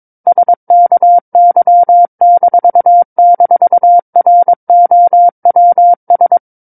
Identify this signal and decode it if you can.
Morse code, heard as a keyed tone, about 22 wpm, 700 Hz. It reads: SKY--ROWS
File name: Book One / 13-Chapter 13 - 22 WPM